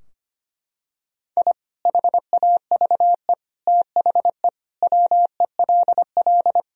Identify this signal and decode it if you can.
Morse code, heard as a keyed tone, about 25 wpm, 700 Hz. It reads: I HAVE THE WELL
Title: Sentences from Top 1000 Words - Review Part 3 - 25wpm